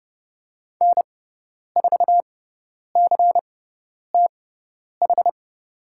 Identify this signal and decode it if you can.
Morse code, heard as a keyed tone, about 30 wpm, spaced as for 10 wpm, 700 Hz. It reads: N4CTH
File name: Lesson 16g - Letter c - Call Signs 30wpm